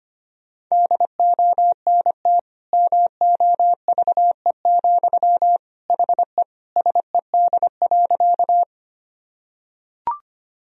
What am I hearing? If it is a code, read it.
Morse code, heard as a keyed tone, about 25 wpm, 700 Hz. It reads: DONT MOVE, HE SED.
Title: Sentences from Top 300 Words and Abbreviations 25wpm